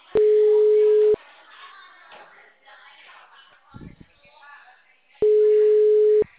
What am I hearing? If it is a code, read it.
Ringback tone (Europe, 425 Hz).